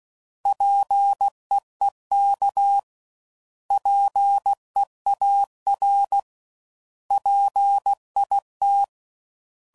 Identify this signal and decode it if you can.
Morse code, heard as a keyed tone, about 16 wpm, 800 Hz. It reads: PEEK PEAR PIT
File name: Prefixes / By First Letter / Ps